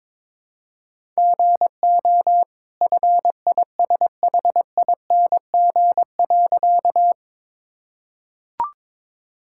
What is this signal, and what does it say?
Morse code, heard as a keyed tone, about 22 wpm, 690 Hz. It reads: GO FISHING.
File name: practice / 22wpm / Easy Sentences with Hints - 22wpm